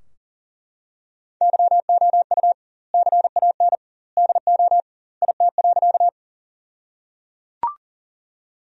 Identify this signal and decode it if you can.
Morse code, heard as a keyed tone, about 37 wpm, 700 Hz. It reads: YOU CAN DO IT.
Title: 2x Word-Spacing - Sentences from Top 100 Words - 40wpm